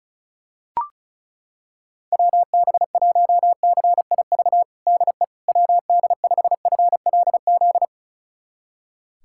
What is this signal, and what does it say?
Morse code, heard as a keyed tone, about 36 wpm, 700 Hz. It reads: WB1CIV DE WD5FLZ
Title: USA Callsign DE Callsign - 6 Characters - 35wpm